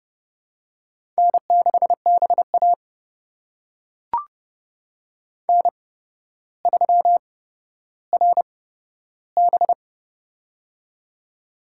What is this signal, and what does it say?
Morse code, heard as a keyed tone, about 30 wpm, spaced as for 13 wpm, 700 Hz. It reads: N6BA N3RB
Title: USA Call Signs - 4 Characters - 8x Character Spacing - 30wpm